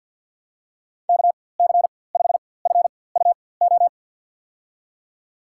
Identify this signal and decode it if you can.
Morse code, heard as a keyed tone, about 53 wpm, spaced as for 24 wpm, 700 Hz. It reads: KC5FUO
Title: USA Call Signs - 4x Character Spacing - 50wpm